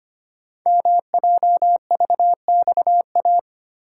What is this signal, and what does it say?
Morse code, heard as a keyed tone, about 25 wpm, 700 Hz. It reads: MJVXA
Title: Code Groups - 5 Letters - 25wpm